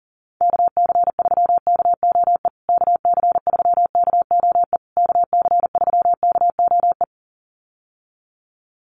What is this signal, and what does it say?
Morse code, heard as a keyed tone, about 37 wpm, 700 Hz. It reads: KC3KOE KC3KOE KC3KOE
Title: USA Call Signs Repeated 3x 40wpm